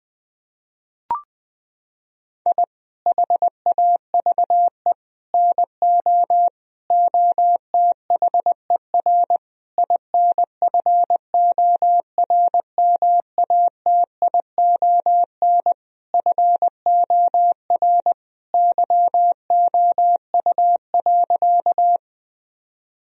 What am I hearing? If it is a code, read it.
Morse code, heard as a keyed tone, about 21 wpm, 700 Hz. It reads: I HAVE NO OTHER INFORMATION FOR YOU.
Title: Sentences from Top 1000 Words - Review Part 3 - 20wpm